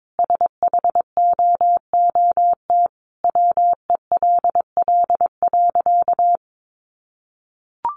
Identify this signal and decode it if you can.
Morse code, heard as a keyed tone, about 22 wpm, 700 Hz. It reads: SHOOT WELL.
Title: Sentences from Top 1200 Words 22wpm